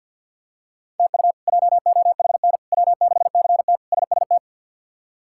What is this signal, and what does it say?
Morse code, heard as a keyed tone, about 46 wpm, 700 Hz. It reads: TU JOHN W6CT SST
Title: K1USN SST QSO Practice - 50wpm